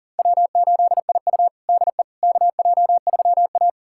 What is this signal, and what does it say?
Morse code, heard as a keyed tone, about 40 wpm, 700 Hz. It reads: W8IU DE KJ3A
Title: USA Callsign DE Callsign - 4 Characters - 40wpm